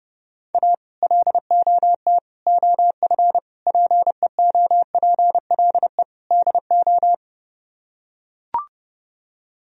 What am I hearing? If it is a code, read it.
Morse code, heard as a keyed tone, about 29 wpm, 700 Hz. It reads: A LOT OF PEOPLE DO